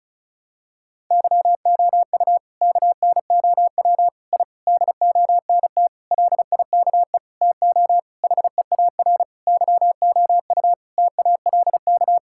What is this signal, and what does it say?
Morse code, heard as a keyed tone, about 36 wpm, 700 Hz. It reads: YOU KNOW I DONT LIKE TO HEAR YOU TALK
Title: Sentences from Top 1000 Words - Review Part 6 - 35wpm